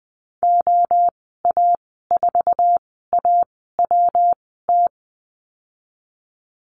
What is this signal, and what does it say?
Morse code, heard as a keyed tone, about 19 wpm, spaced as for 14 wpm, 700 Hz. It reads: OA4AWT
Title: 122 Lesson 20g - Number 3 - Call Signs 20wpm